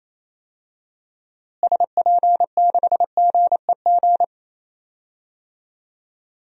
Morse code, heard as a keyed tone, about 28 wpm, 700 Hz. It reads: SP6GEG
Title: DXCC-100 Call Signs - 6 Characters - Repeated 3x - 28wpm